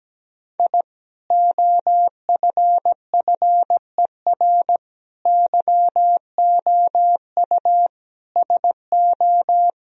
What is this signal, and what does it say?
Morse code, heard as a keyed tone, about 17 wpm, 700 Hz. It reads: I OFFER YOU SO
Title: Sentences from Top 1000 Words - Review Part 4 - 17wpm